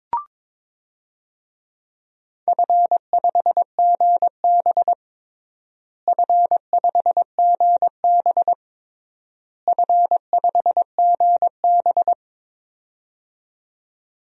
Morse code, heard as a keyed tone, about 22 wpm, 700 Hz. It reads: F5GB F5GB F5GB
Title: DXCC-100 Call Signs - 4 Characters - Repeated 3x - 22wpm